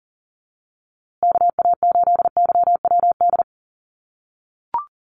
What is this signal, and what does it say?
Morse code, heard as a keyed tone, about 40 wpm, 700 Hz. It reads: KA8YWD
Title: USA Call Signs Repeated 2x 40wpm